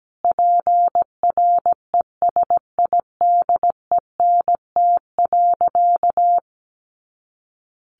Morse code, heard as a keyed tone, about 17 wpm, 700 Hz. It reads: PRESIDENT.
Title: A Sound Of Thunder 17wpm